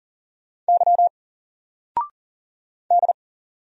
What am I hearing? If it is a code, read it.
Morse code, heard as a keyed tone, about 40 wpm, 700 Hz. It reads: Y D
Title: Single Letter-Number 40wpm